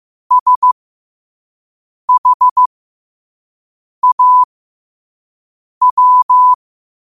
Morse code, heard as a keyed tone, about 15 wpm, spaced as for 5 wpm, 1000 Hz. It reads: SHAW
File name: qotdfiles / 5wpm